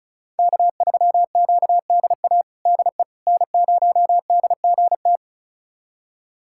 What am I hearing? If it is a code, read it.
Morse code, heard as a keyed tone, about 36 wpm, 700 Hz. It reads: K3QDA DE N0DGT
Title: USA Callsign DE Callsign - 5 Characters - 35wpm